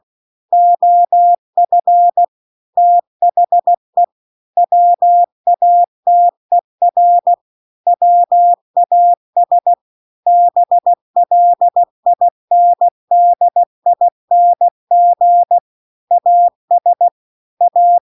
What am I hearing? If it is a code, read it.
Morse code, heard as a keyed tone, about 16 wpm, 700 Hz. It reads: OF THE WATER WAS BLINDING AS A